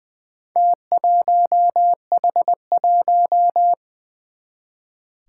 Morse code, heard as a keyed tone, about 21 wpm, 700 Hz. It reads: T1H1
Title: Four Challenging Letter-Number Combinations 20wpm